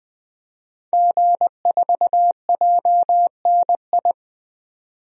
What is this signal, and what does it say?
Morse code, heard as a keyed tone, about 20 wpm, 700 Hz. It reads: G4JNI
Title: DXCC-100 Call Signs - 5 Characters - 20wpm